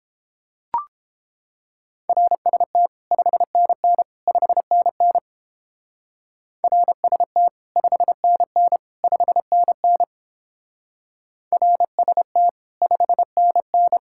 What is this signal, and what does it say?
Morse code, heard as a keyed tone, about 29 wpm, 700 Hz. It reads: RST 5NN 5NN RST 5NN 5NN RST 5NN